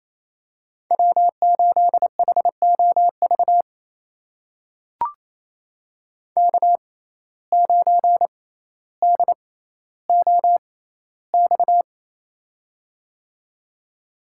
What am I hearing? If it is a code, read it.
Morse code, heard as a keyed tone, about 28 wpm, spaced as for 14 wpm, 700 Hz. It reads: W8HOV K9DOX
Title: USA Call Signs - 5 Characters - 6x Character Spacing - 28wpm